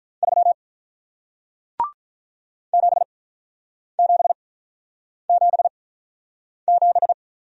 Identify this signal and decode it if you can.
Morse code, heard as a keyed tone, about 46 wpm, spaced as for 9 wpm, 700 Hz. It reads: 3 7777